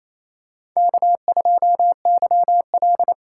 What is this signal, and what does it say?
Morse code, heard as a keyed tone, about 28 wpm, 700 Hz. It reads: K2YL
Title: USA Call Signs - 4 Characters - 2x Character Spacing - 28wpm